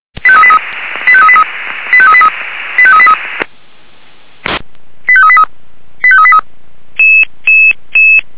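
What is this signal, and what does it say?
Five-tone selective call (ZVEI), heard as quick runs of five tones, tones of about 70 ms. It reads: 85383 85383 85383 85383 85383 85383
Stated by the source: ZVEI five-tone call 85383 received with an SDR.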